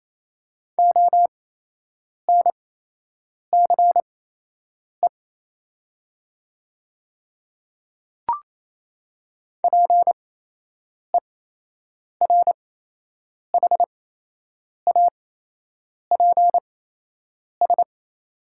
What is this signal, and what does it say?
Morse code, heard as a keyed tone, about 28 wpm, spaced as for 8 wpm, 700 Hz. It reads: ONCE PERHAPS
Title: 8x Character Spacing - Top 500 Words - 28wpm